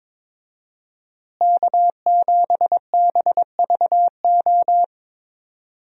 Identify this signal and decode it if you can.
Morse code, heard as a keyed tone, about 22 wpm, 700 Hz. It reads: K7BVO
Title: USA Call Signs - 5 Characters - Repeated 2x - 22wpm